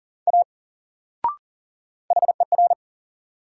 Morse code, heard as a keyed tone, about 40 wpm, 700 Hz. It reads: A HER
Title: Sentences from Top 100 Words - New Word Drill - 40wpm